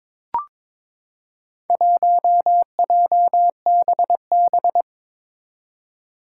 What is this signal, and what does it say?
Morse code, heard as a keyed tone, about 21 wpm, 700 Hz. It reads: 1JBB
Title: Four Challenging Letter-Number Combinations 22wpm